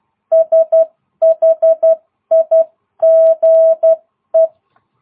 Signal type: Morse code